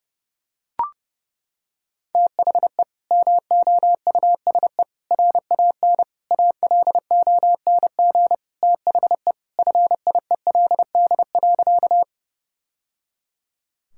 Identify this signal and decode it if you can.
Morse code, heard as a keyed tone, about 30 wpm, 700 Hz. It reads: THE MOUSE RAN ALONG THE FIELD.
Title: Sentences from Top 1000 Words - Review Part 5 - 30wpm